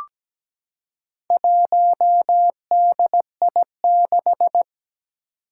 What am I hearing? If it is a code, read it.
Morse code, heard as a keyed tone, about 17 wpm, 700 Hz. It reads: 1DI6
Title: Four Challenging Letter-Number Combinations 17wpm